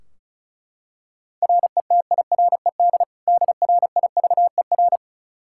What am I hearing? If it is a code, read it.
Morse code, heard as a keyed tone, about 36 wpm, 700 Hz. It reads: RETIRED DRIVER
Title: Practice QSO - Exchange 3 - More Info - 35wpm